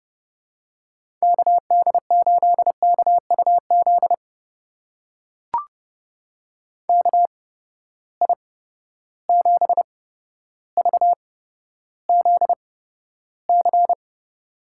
Morse code, heard as a keyed tone, about 30 wpm, spaced as for 13 wpm, 700 Hz. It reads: KD8KUZ KI7VZC
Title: USA Call Signs - 6 Characters - 8x Character Spacing - 30wpm